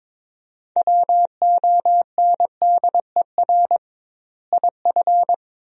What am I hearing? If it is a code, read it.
Morse code, heard as a keyed tone, about 22 wpm, 700 Hz. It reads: WONDER IF
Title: 2x Word-Spacing - Sentences from Top 500 Words - 22wpm